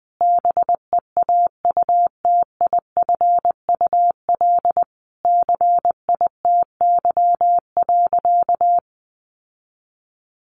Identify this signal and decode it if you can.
Morse code, heard as a keyed tone, about 20 wpm, 700 Hz. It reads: BEAUTIFUL CITY.
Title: Sentences from Top 1100 Words 20wpm